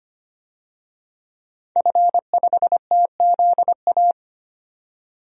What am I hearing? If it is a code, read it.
Morse code, heard as a keyed tone, about 25 wpm, 700 Hz. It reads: F5TZA
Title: DXCC-100 Call Signs - 5 Characters - 25wpm